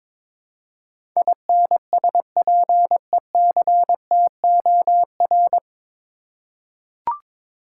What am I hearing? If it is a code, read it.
Morse code, heard as a keyed tone, about 22 wpm, 700 Hz. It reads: INSPECTOR
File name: 9 Letter Words - First 3 Letters Given - 22wpm